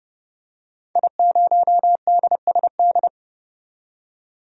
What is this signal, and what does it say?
Morse code, heard as a keyed tone, about 31 wpm, 700 Hz. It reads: I0DSD
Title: DXCC-100 Call Signs - 5 Characters - 30wpm